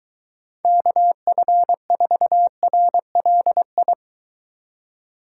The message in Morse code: KF4RLI

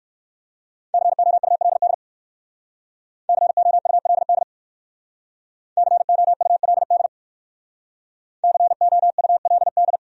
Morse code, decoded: COULD COULD COULD COULD

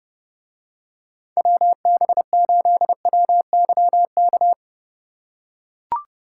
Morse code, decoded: WB8WYK

